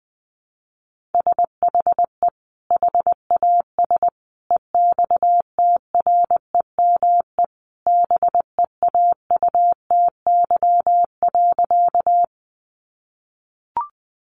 Morse code, heard as a keyed tone, about 20 wpm, 700 Hz. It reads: SHE HAS EXTREME BEAUTY.
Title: Sentences from Top 1300 Words 20wpm